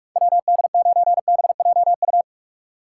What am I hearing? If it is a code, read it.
Morse code, heard as a keyed tone, about 46 wpm, 690 Hz. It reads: WD9BJU
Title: USA Call Signs - 6 Characters - 3x Character Spacing - 45wpm